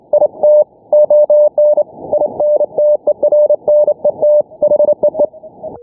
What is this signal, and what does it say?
Morse code, heard as a keyed tone, about 28 wpm, 600 Hz. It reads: IT ON INTERNET HEE
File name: n6tr / HomeBrewTransmitter / N6TR 0120z 7019kHz